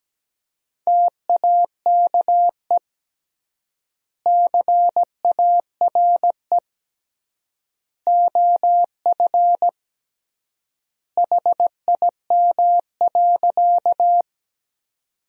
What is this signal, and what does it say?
Morse code, heard as a keyed tone, about 17 wpm, 700 Hz. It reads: TAKE CARE OF HIM.
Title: 3x Word-Spacing - Sentences from Top 300 Words - 17wpm